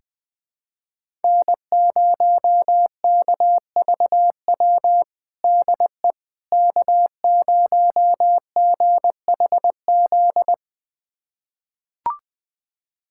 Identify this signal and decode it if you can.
Morse code, heard as a keyed tone, about 20 wpm, 700 Hz. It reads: N0KVW DE K0GHZ
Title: USA Callsign DE Callsign - 5 Characters - 20wpm